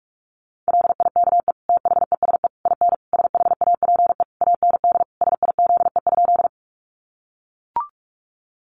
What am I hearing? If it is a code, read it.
Morse code, heard as a keyed tone, about 44 wpm, 700 Hz. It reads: LIKE THESE IN SHAPE AND SIZE?
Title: Sentences from Top 500 Words 45wpm